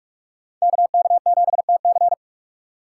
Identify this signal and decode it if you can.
Morse code, heard as a keyed tone, about 43 wpm, 700 Hz. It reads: KK7TC